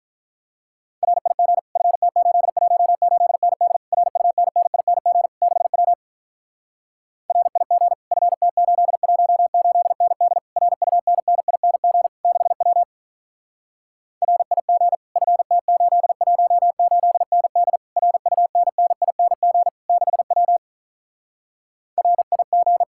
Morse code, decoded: RIG FT818ND RUNNING 6W RIG FT818ND RUNNING 6W RIG FT818ND RUNNING 6W RIG

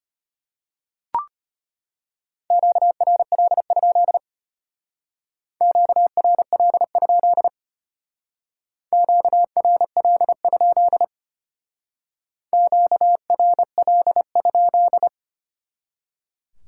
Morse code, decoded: QRL? QRL? QRL? QRL?

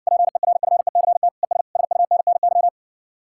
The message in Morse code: PERFECT ES SUNNY

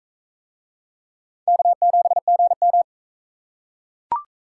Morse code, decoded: K7GM

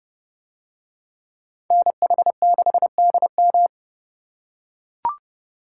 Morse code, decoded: NH6DM

